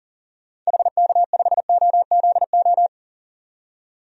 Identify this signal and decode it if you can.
Morse code, heard as a keyed tone, about 40 wpm, 700 Hz. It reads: HK5OZO
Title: DXCC-100 Call Signs - 6 Characters - 40wpm